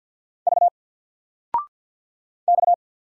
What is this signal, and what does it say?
Morse code, heard as a keyed tone, about 49 wpm, 700 Hz. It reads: V X
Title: Single Letter-Number 50wpm